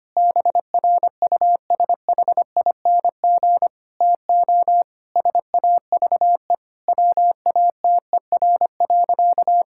Morse code, heard as a keyed tone, about 25 wpm, 700 Hz. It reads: BRUSHING TO SAVE WATER.
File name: practice / 25wpm / Difficult Sentences with Hints - 25wpm